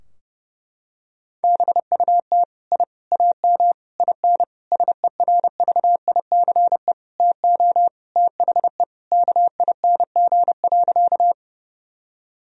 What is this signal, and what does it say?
Morse code, heard as a keyed tone, about 30 wpm, 700 Hz. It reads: BUT I AM IN SERVICE TO THE KING.
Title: Sentences from Top 1000 Words - Review Part 4 - 30wpm